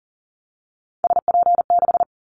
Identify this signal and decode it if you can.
Morse code, heard as a keyed tone, about 40 wpm, 700 Hz. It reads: SP6
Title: Callsign Prefixes for Top 25 DXCC - Repeated 3x - 40wpm